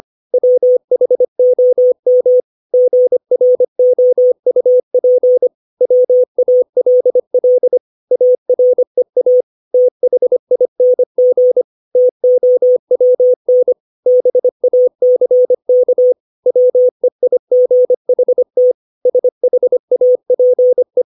Morse code, heard as a keyed tone, about 25 wpm, 500 Hz. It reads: WHOM GROUP WALL AREA THING TOWN BACK WEIGHT SHAPE
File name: 632 words / 632 Words 25WPM